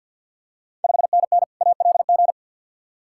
Morse code, heard as a keyed tone, about 51 wpm, 690 Hz. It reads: 5NN ALG
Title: Ontario QSO Party Exchange 50wpm